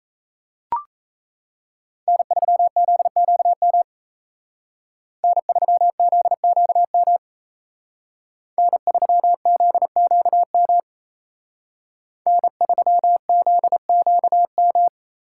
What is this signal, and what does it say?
Morse code, heard as a keyed tone, about 36 wpm, 700 Hz. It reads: N3ZQM N3ZQM N3ZQM N3ZQM